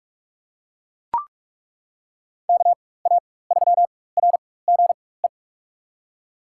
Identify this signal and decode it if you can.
Morse code, heard as a keyed tone, about 42 wpm, spaced as for 21 wpm, 700 Hz. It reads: KA3RGE
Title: USA Call Signs - 4x Character Spacing - 45wpm